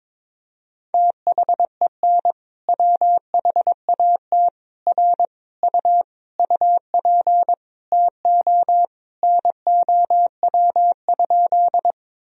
Morse code, heard as a keyed tone, about 22 wpm, 700 Hz. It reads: THEN WHAT R U UP TO NOW?
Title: Sentences from Top 100 Words and Abbreviations 22wpm